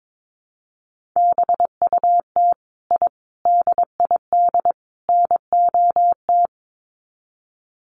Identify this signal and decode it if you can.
Morse code, heard as a keyed tone, about 22 wpm, 700 Hz. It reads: BUT I DID NOT
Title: Sets of 4 Words 22wpm